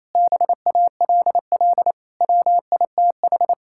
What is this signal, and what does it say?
Morse code, heard as a keyed tone, about 28 wpm, 700 Hz. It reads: BALL WITH
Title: Sentences from Top 1000 Words - Review Part 6 - 28wpm